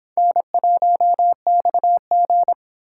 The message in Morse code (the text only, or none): N1XG